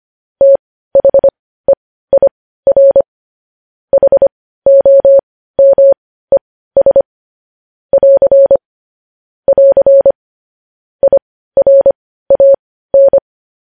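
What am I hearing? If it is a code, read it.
Morse code, heard as a keyed tone, about 25 wpm, spaced as for 15 wpm, 550 Hz. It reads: THEIR HOMES <AR> <AR> IRAN
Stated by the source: News headlines in Morse code at 15 words per minute, characters sent at 25 WPM.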